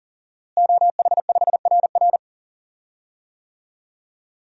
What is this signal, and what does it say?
Morse code, heard as a keyed tone, about 37 wpm, 700 Hz. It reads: OH5RR
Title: DXCC-100 Call Signs - 5 Characters - Repeated 3x - 40wpm